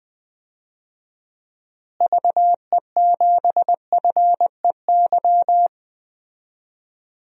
Morse code, heard as a keyed tone, about 20 wpm, 700 Hz. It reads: VE7FEY
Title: DXCC-100 Call Signs - 6 Characters - Repeated 3x - 20wpm